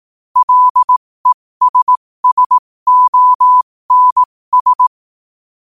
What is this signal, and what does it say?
Morse code, heard as a keyed tone, about 18 wpm, spaced as for 15 wpm, 1000 Hz. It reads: LESSONS